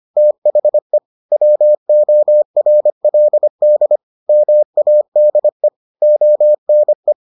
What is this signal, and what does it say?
Morse code, heard as a keyed tone, about 25 wpm, 600 Hz. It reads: THE WORLD MADE ONE